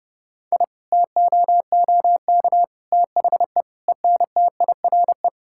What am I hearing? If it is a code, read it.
Morse code, heard as a keyed tone, about 32 wpm, 700 Hz. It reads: I TOOK THE ENTIRE
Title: Sentences from Top 1000 Words - Review Part 5 - 30wpm